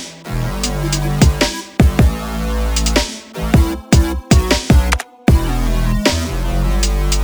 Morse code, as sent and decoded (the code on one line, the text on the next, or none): none